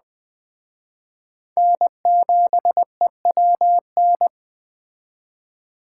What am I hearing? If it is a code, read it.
Morse code, heard as a keyed tone, about 21 wpm, 700 Hz. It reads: N7EWN
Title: Speed Racing - USA 1x3 Call Signs 17wpm